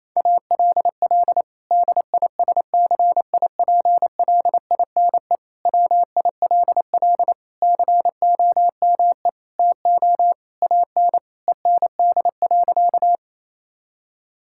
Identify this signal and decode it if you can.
Morse code, heard as a keyed tone, about 27 wpm, 700 Hz. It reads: ALL DISCIPLINE WILL COME TO AN END.
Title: Sentences from Top 1000 Words - Review Part 6 - 28wpm